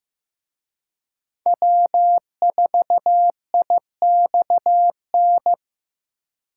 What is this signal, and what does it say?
Morse code, heard as a keyed tone, about 18 wpm, 700 Hz. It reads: W4IXN
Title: Speed Racing - USA 1x3 Call Signs 15wpm